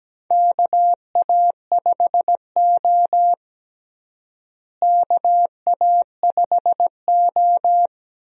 Morse code, heard as a keyed tone, about 17 wpm, 700 Hz. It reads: KA5O KA5O
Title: USA Call Signs - 4 Characters - Repeated 2x - 17wpm